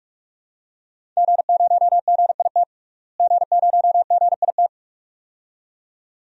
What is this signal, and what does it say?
Morse code, heard as a keyed tone, about 46 wpm, 700 Hz. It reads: G0GIT G0GIT
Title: DXCC-100 Call Signs - 5 Characters - Repeated 2x - 45wpm